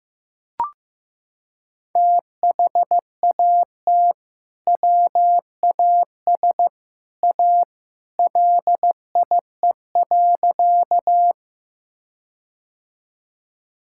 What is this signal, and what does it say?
Morse code, heard as a keyed tone, about 15 wpm, 700 Hz. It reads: THAT WAS A LIE.
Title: Sentences from Top 1600 Words 15wpm